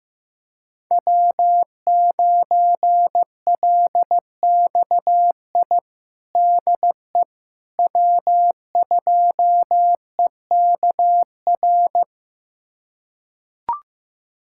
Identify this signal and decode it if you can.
Morse code, heard as a keyed tone, about 15 wpm, 700 Hz. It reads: W9LXI DE W2EKR
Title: USA Callsign DE Callsign - 5 Characters - 15wpm